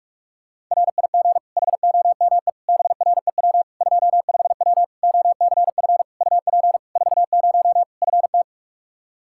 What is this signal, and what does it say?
Morse code, heard as a keyed tone, about 46 wpm, 700 Hz. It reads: RIG HOME BREW 25W OCF UP 40 FT
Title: Practice QSO - Exchange 2 - Info - 45wpm